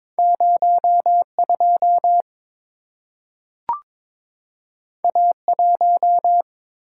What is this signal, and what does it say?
Morse code, heard as a keyed tone, about 22 wpm, 690 Hz. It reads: 02 A1
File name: practice / 22wpm / Two Letter-Number Combinations - 22wpm